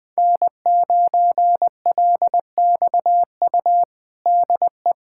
Morse code, heard as a keyed tone, about 20 wpm, 700 Hz. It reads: N9LXU DE
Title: USA Callsign DE Callsign - 5 Characters - 20wpm